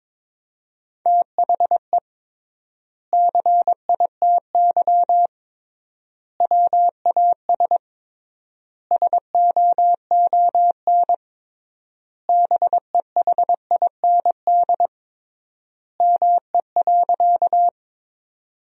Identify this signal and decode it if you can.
Morse code, heard as a keyed tone, about 22 wpm, 700 Hz. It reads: THE CITY WAS SOON BEHIND ME.
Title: 3x Word-Spacing - Sentences from Top 500 Words - 22wpm